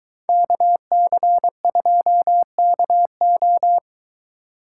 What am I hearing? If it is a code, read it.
Morse code, heard as a keyed tone, about 22 wpm, 700 Hz. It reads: KC2KO